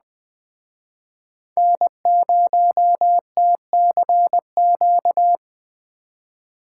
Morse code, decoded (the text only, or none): N0TCQ